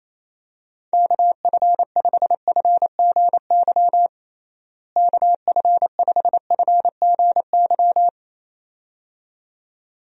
Morse code, heard as a keyed tone, about 28 wpm, 700 Hz. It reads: KF5FGY KF5FGY
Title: USA Call Signs - 6 Characters - Repeated 2x - 28wpm